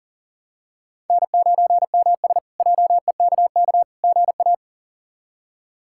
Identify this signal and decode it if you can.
Morse code, heard as a keyed tone, about 40 wpm, 700 Hz. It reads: N9MS JEKK GA
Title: K1USN SST QSO Practice - 40wpm